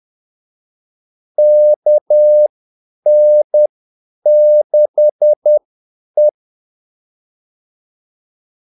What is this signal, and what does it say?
Morse code, heard as a keyed tone, about 10 wpm, spaced as for 8 wpm, 600 Hz. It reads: KN6E
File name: Callsigns-10-8wpm